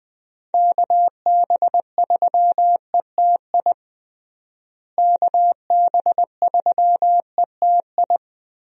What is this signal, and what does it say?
Morse code, heard as a keyed tone, about 20 wpm, 700 Hz. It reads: KB3ETI KB3ETI
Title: USA Call Signs - 6 Characters - Repeated 3x - 20wpm